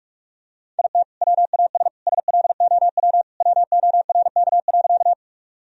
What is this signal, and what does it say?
Morse code, heard as a keyed tone, about 46 wpm, 700 Hz. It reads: IT WAS SLOW WORK.
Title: Sentences from Top 1000 Words - Review Part 2 - 45wpm